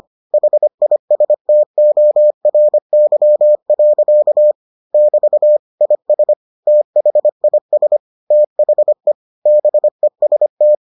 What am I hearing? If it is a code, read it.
Morse code, heard as a keyed tone, about 25 wpm, 600 Hz. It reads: HISTORY. <BT> IS THIS THE BEST